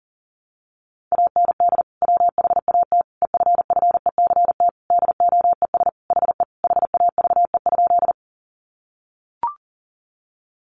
Morse code, heard as a keyed tone, about 40 wpm, 700 Hz. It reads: AND WHAT EFFECT DOES HE HAVE?
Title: Sentences from Top 1000 Words 40wpm